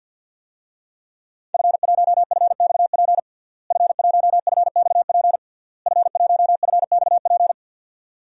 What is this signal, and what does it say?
Morse code, heard as a keyed tone, about 52 wpm, 700 Hz. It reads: F1FXP F1FXP F1FXP
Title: DXCC-100 Call Signs - 5 Characters - Repeated 3x - 50wpm